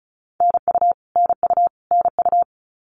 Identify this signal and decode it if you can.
Morse code, heard as a keyed tone, about 34 wpm, 700 Hz. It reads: NU NU NU
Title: Canadian Provinces and Territories - Repeated 3x - 35wpm